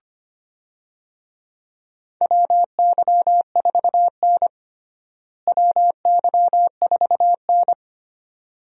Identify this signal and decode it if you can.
Morse code, heard as a keyed tone, about 25 wpm, 700 Hz. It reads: WY4N WY4N
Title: USA Call Signs - 4 Characters - Repeated 2x - 25wpm